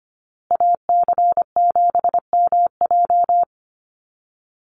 Morse code, heard as a keyed tone, about 25 wpm, 700 Hz. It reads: AC7MJ
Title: USA Call Signs Repeated 2x 25wpm